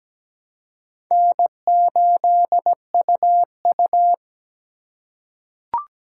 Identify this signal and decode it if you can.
Morse code, heard as a keyed tone, about 17 wpm, 700 Hz. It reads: N8UU
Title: USA Call Signs - 4 Characters - 4x Character Spacing - 17wpm